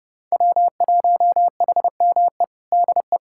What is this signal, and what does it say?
Morse code, heard as a keyed tone, about 30 wpm, 700 Hz. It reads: W1HME DE
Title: USA Callsign DE Callsign - 5 Characters - 30wpm